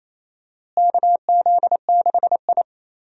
Morse code, heard as a keyed tone, about 28 wpm, 700 Hz. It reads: KZ6I
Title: USA Call Signs - 4 Characters - Repeated 3x - 28wpm